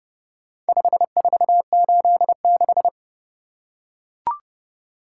Morse code, heard as a keyed tone, about 29 wpm, 700 Hz. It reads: 5486